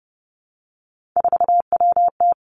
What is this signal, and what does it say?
Morse code, heard as a keyed tone, about 30 wpm, 700 Hz. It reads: 4WT
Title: Three Challenging Letter-Number Combinations 30wpm